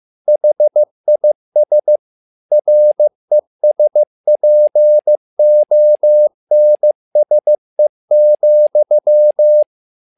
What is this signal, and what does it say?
Morse code, heard as a keyed tone, about 15 wpm, 600 Hz. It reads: HIS RESPONSE,
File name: learning center / tower trouble / tower trouble 15wpm